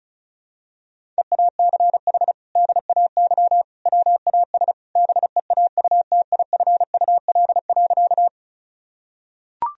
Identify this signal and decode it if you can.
Morse code, heard as a keyed tone, about 34 wpm, 700 Hz. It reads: EACH DAY WAS BEAUTIFUL.
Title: Sentences from Top 1000 Words - Review Part 1 - 35wpm